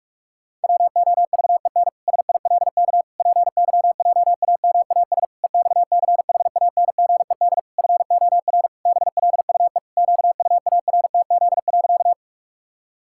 Morse code, heard as a keyed tone, about 46 wpm, 700 Hz. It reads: WOVEN SILK PYJAMAS EXCHANGED FOR BLUE QUARTZ.
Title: Pangram Challenge v2 - 45wpm